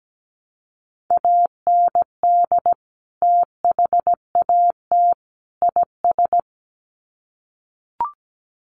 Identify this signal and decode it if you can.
Morse code, heard as a keyed tone, about 17 wpm, 700 Hz. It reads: AND THAT IS
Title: Sets of 3 Words 17wpm